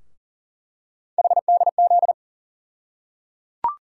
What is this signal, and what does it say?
Morse code, heard as a keyed tone, about 38 wpm, 700 Hz. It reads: HDZ
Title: USA Call Sign Postfixes - 3 Letters 40wpm